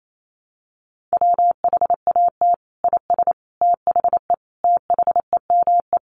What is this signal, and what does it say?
Morse code, heard as a keyed tone, about 28 wpm, 700 Hz. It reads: WHAT IS THE THEME
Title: Sentences from Top 1100 Words 28wpm